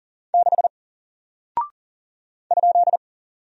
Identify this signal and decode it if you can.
Morse code, heard as a keyed tone, about 41 wpm, 700 Hz. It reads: 6 ?
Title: Single Character (Letter:Number:Punctuation:Prosign) Alt 40wpm